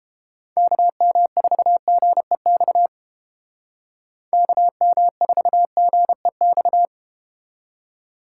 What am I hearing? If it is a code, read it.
Morse code, heard as a keyed tone, about 32 wpm, 700 Hz. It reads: KM4GEX KM4GEX